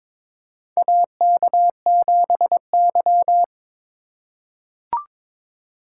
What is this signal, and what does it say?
Morse code, heard as a keyed tone, about 22 wpm, 700 Hz. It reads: AK7Y
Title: USA Call Signs - 4 Characters - 3x Character Spacing - 22wpm